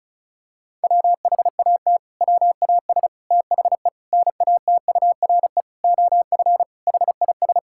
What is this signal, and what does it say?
Morse code, heard as a keyed tone, about 34 wpm, 700 Hz. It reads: WHAT WAS THE NATURE OF HIS
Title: Sentences from Top 1600 Words 35wpm